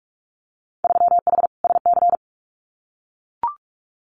Morse code, heard as a keyed tone, about 44 wpm, 700 Hz. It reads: 3H SC
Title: Winter Field Day Exchanges 45wpm